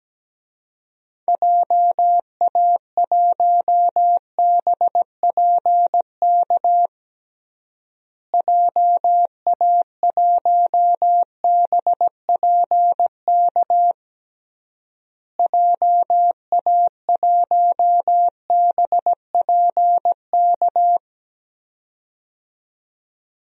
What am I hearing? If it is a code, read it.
Morse code, heard as a keyed tone, about 17 wpm, 700 Hz. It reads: JA1BPK JA1BPK JA1BPK
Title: DXCC-100 Call Signs - 6 Characters - Repeated 3x - 17wpm